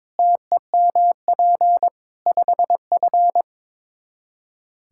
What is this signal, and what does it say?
Morse code, heard as a keyed tone, about 22 wpm, 700 Hz. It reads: TEMP 5F
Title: Practice QSO - Exchange 2 - Info - 22wpm